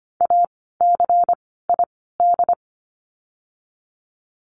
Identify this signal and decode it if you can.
Morse code, heard as a keyed tone, about 25 wpm, 700 Hz. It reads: A C I D
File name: course / 25:10wpm / 094 Lesson 16f - Letter c - Words 25wpm